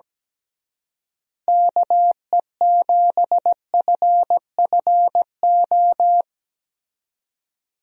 Morse code, heard as a keyed tone, about 17 wpm, 700 Hz. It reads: KE7FFO
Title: USA Call Signs - 6 Characters - Repeated 3x - 17wpm